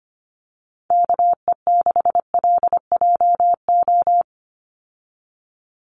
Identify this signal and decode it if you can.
Morse code, heard as a keyed tone, about 25 wpm, 700 Hz. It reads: KE6LJO
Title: USA Call Signs 25wpm